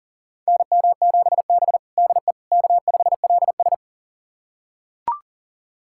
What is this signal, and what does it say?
Morse code, heard as a keyed tone, about 40 wpm, 700 Hz. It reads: NM7B DE K5LS
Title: USA Callsign DE Callsign - 4 Characters - 40wpm